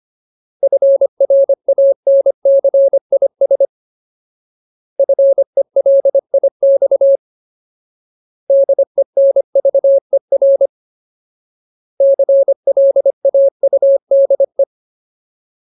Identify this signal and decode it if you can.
Morse code, heard as a keyed tone, about 25 wpm, 550 Hz. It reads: FRANCIS FELIX DENVER CLAUDE
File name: Ultimate Word List / 25WPM Ultimate